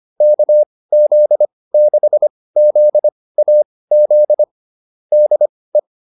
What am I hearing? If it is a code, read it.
Morse code, heard as a keyed tone, about 25 wpm, spaced as for 18 wpm, 600 Hz. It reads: KZ6ZAZ DE
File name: QSO302 18